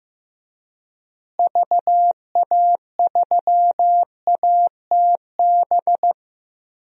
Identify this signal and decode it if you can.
Morse code, heard as a keyed tone, about 15 wpm, 700 Hz. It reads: VA3ATB
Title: DXCC-100 Call Signs - 6 Characters - Repeated 3x - 15wpm